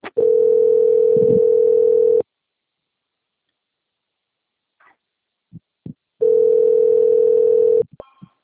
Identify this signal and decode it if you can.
Ringback tone (North America), 440 + 480 Hz.